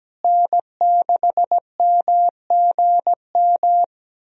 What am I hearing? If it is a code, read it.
Morse code, heard as a keyed tone, about 17 wpm, 700 Hz. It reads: N6MGM
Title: USA Call Signs - 5 Characters - Repeated 3x - 17wpm